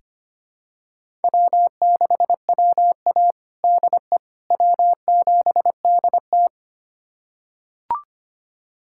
Morse code, heard as a keyed tone, about 25 wpm, 700 Hz. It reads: W6WA DE W7DT
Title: USA Callsign DE Callsign - 4 Characters - 25wpm